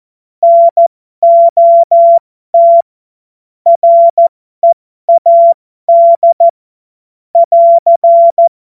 Morse code, heard as a keyed tone, about 14 wpm, spaced as for 12 wpm, 680 Hz. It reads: NOT READ <AR>
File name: Morsekurs / Witze